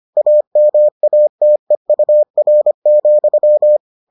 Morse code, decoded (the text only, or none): AMATEUR,